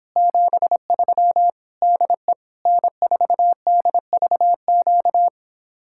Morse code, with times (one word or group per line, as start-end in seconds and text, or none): 0.2-1.5 73
1.8-2.3 DE
2.7-5.3 N4DVQ